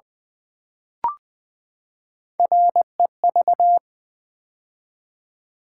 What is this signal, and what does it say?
Morse code, heard as a keyed tone, about 20 wpm, 700 Hz. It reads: REV
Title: Build A Word - 3 Letters At A Time - 8-Letter Words - 20wpm